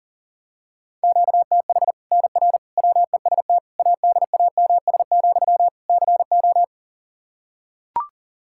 Morse code, heard as a keyed tone, about 40 wpm, 700 Hz. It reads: QTH NR WEST ADAMS, CO
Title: Practice QSO - Exchange 1 - Beginning - 40wpm